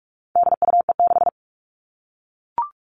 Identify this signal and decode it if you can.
Morse code, heard as a keyed tone, about 44 wpm, 700 Hz. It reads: DUE6
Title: All Cut Numbers - 4-Digit 45wpm